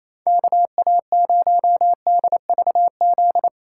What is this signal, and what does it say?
Morse code, heard as a keyed tone, about 28 wpm, 700 Hz. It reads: KA0DVZ
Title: USA Call Signs - 6 Characters - 4x Character Spacing - 28wpm